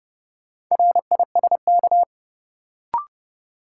Morse code, heard as a keyed tone, about 30 wpm, 700 Hz. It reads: RISK
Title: Sentences from Top 600 Words - New Word Drill - 30wpm